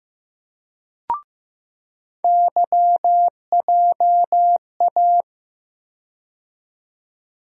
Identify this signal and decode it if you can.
Morse code, heard as a keyed tone, about 15 wpm, 700 Hz. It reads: YJA
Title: USA Call Sign Postfixes - 3 Letters 15wpm